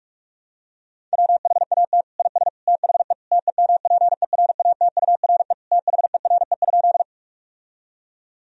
Morse code, heard as a keyed tone, about 46 wpm, 700 Hz. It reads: WHAT IS THE TEMPERATURE THERE?
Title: Sentences from Top 1000 Words - Review Part 4 - 45wpm